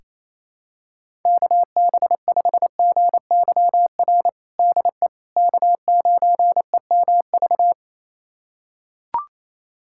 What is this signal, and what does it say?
Morse code, heard as a keyed tone, about 28 wpm, 700 Hz. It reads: KB5GYR DE K9EMV
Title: USA Callsign DE Callsign 28wpm